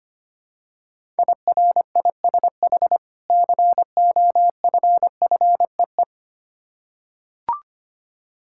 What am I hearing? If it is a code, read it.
Morse code, heard as a keyed tone, about 25 wpm, 700 Hz. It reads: IRISH COFFEE
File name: practice / 25wpm / Difficult Words with Hints - 25wpm